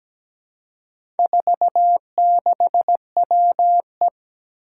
Morse code, heard as a keyed tone, about 17 wpm, 700 Hz. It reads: 46WE
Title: Four Challenging Letter-Number Combinations 17wpm